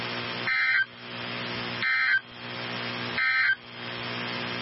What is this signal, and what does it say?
Emergency Alert System (SAME) tones, heard as harsh buzzing bursts. It is heard from 0.4 to 3.6 s.
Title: NOAA Weather Radio 162.550